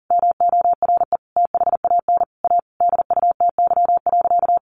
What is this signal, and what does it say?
Morse code, heard as a keyed tone, about 40 wpm, 700 Hz. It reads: MORE THAN A DUTY.
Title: Sentences from Top 1500 Words 40wpm